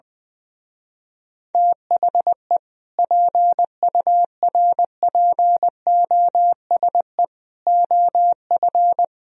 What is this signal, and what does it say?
Morse code, heard as a keyed tone, about 21 wpm, 700 Hz. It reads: THE PURPOSE OF
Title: Speed-Racing - Sets of 5 Words 20wpm